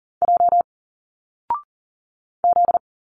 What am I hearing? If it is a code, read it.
Morse code, heard as a keyed tone, about 40 wpm, 700 Hz. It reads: J Z
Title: Single Letters 40wpm